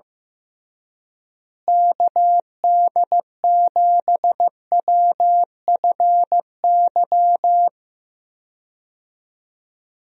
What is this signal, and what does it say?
Morse code, heard as a keyed tone, about 15 wpm, 700 Hz. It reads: KD7WFY
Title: USA Call Signs - 6 Characters - Repeated 2x - 15wpm